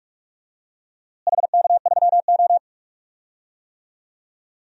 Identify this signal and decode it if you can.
Morse code, heard as a keyed tone, about 48 wpm, 700 Hz. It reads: HK3O
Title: DXCC-100 Call Signs - 4 Characters - Repeated 2x - 45wpm